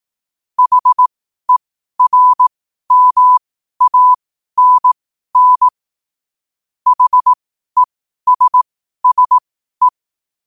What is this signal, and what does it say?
Morse code, heard as a keyed tone, about 18 wpm, spaced as for 12 wpm, 1000 Hz. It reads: HERMANN HESSE
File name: qotdfiles / 12wpm